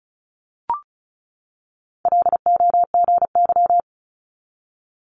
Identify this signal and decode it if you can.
Morse code, heard as a keyed tone, about 36 wpm, 700 Hz. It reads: LOGY
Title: Multi-letter Phonemes 35wpm